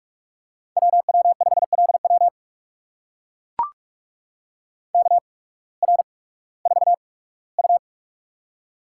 Morse code, heard as a keyed tone, about 44 wpm, spaced as for 21 wpm, 700 Hz. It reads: WW5LW KR4U
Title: USA Call Signs - 8x Character Spacing - 45wpm